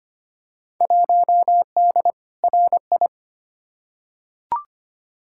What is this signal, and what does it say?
Morse code, heard as a keyed tone, about 25 wpm, 700 Hz. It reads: 1D RI
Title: Field Day Exchanges - 25wpm